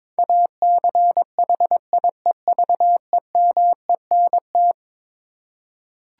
Morse code, heard as a keyed tone, about 21 wpm, 700 Hz. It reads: ACHIEVEMENT
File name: practice / 22wpm / Eleven Letter Words - 22wpm